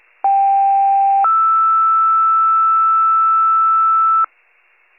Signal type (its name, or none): two-tone page